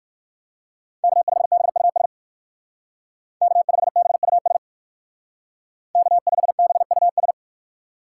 Morse code, heard as a keyed tone, about 55 wpm, 700 Hz. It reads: K5BUS K5BUS K5BUS
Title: Speed Racing - USA 1x3 Call Signs 45wpm